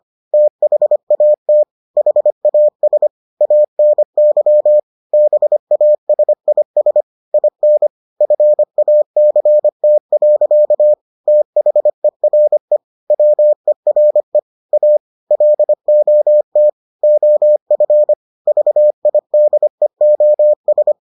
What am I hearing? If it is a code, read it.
Morse code, heard as a keyed tone, about 25 wpm, 600 Hz. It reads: THAT HAS ANY BASIS IN FACT. THERE WERE A LOT OF VIDEOS